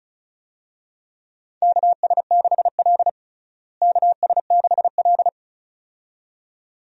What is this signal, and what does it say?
Morse code, heard as a keyed tone, about 36 wpm, 700 Hz. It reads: KS6L KS6L
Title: USA Call Signs - 4 Characters - Repeated 2x - 35wpm